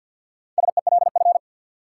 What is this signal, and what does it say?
Morse code, heard as a keyed tone, about 52 wpm, 700 Hz. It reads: SELF
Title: Sentences from Top 1000 Words - Review Part 1 - New Word Drill - 50wpm